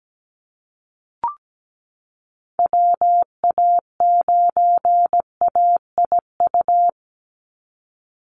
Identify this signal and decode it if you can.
Morse code, heard as a keyed tone, about 17 wpm, 700 Hz. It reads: WA9AIU
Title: USA Call Signs 17wpm